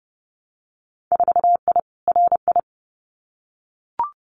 Morse code, heard as a keyed tone, about 30 wpm, 700 Hz. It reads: 4I RI
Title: Winter Field Day Exchanges Encore 30wpm